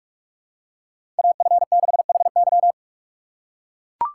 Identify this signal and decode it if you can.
Morse code, heard as a keyed tone, about 41 wpm, 700 Hz. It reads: AF6HY
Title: USA Call Signs - 3x Character Spacing - 45wpm